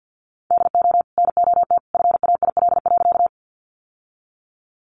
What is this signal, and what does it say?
Morse code, heard as a keyed tone, about 50 wpm, 700 Hz. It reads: DO NOT FAIL.
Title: Sentences from Top 1100 Words 50wpm